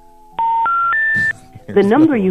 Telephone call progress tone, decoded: special information tone